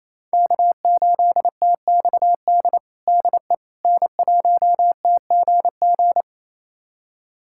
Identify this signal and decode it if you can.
Morse code, heard as a keyed tone, about 28 wpm, 700 Hz. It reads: K8TXD DE N1TGG
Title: USA Callsign DE Callsign - 5 Characters - 28wpm